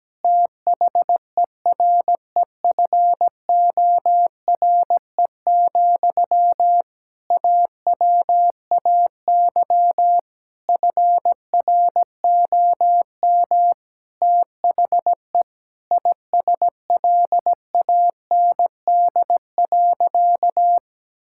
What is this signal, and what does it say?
Morse code, heard as a keyed tone, about 17 wpm, 700 Hz. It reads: THEREFORE, AWAY FROM THE ISLAND.
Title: Sentences from Top 1000 Words - Review Part 5 - 17wpm